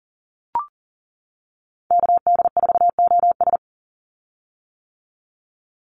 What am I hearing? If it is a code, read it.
Morse code, heard as a keyed tone, about 40 wpm, 700 Hz. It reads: KD4OS
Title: Call Signs - 40wpm